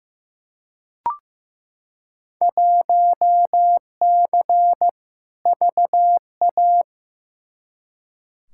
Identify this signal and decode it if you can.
Morse code, heard as a keyed tone, about 15 wpm, 700 Hz. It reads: 1C VA
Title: Field Day Exchanges Encore - 15wpm